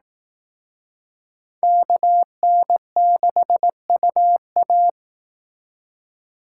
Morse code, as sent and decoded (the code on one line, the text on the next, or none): -.- -. -.... ..- .-
KN6UA